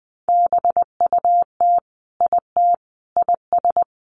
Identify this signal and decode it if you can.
Morse code, heard as a keyed tone, about 20 wpm, 700 Hz. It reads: BUT IT IS
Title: Sentences from Top 1100 Words 20wpm